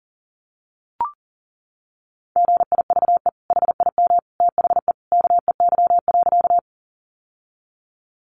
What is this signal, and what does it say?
Morse code, heard as a keyed tone, about 40 wpm, 700 Hz. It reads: GIVE HIM THE KEY.
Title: Sentences from Top 600 Words 40wpm